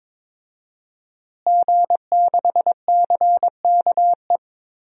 Morse code, heard as a keyed tone, about 22 wpm, 700 Hz. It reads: G6CKE
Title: DXCC-100 Call Signs - 5 Characters - Repeated 2x - 22wpm